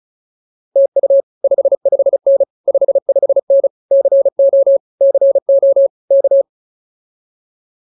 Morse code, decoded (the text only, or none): TU 55N 55N CO CO K